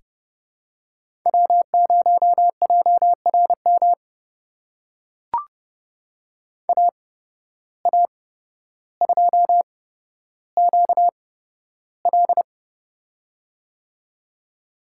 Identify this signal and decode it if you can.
Morse code, heard as a keyed tone, about 30 wpm, spaced as for 13 wpm, 700 Hz. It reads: W0JRM AA2QL
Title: USA Call Signs - 5 Characters - 8x Character Spacing - 30wpm